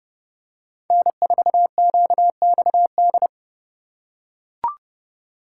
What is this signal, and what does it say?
Morse code, heard as a keyed tone, about 32 wpm, 700 Hz. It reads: N4QXD
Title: USA Call Signs - 5 Characters - 4x Character Spacing - 30wpm